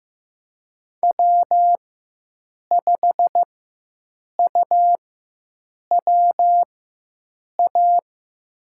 Morse code, heard as a keyed tone, about 15 wpm, spaced as for 7 wpm, 700 Hz. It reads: W5UWA
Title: USA Call Signs - 5 Characters - 4x Character Spacing - 15wpm